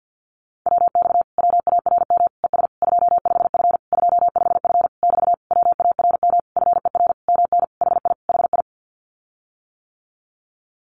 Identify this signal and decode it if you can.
Morse code, heard as a keyed tone, about 50 wpm, 700 Hz. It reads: WX WARM ES 25F 25F <BT> WARM FER MN HI HI
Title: Common QSO Phrases Encore 50wpm